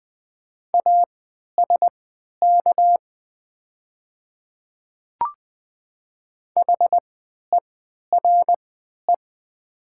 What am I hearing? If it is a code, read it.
Morse code, heard as a keyed tone, about 20 wpm, spaced as for 11 wpm, 700 Hz. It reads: ASK HERE
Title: 3x Character Spacing - Top 200 Words - 20wpm